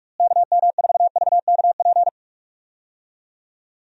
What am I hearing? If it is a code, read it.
Morse code, heard as a keyed tone, about 47 wpm, 700 Hz. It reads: KM4VKP
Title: USA Call Signs - 6 Characters - Repeated 2x - 45wpm